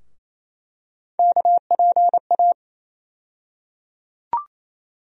Morse code, heard as a keyed tone, about 28 wpm, 700 Hz. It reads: KPA
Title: USA Call Sign Postfixes - 3 Letters 28wpm